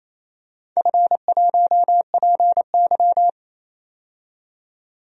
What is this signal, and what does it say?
Morse code, heard as a keyed tone, about 29 wpm, 700 Hz. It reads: F1PY
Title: DXCC-100 Call Signs - 4 Characters - 28wpm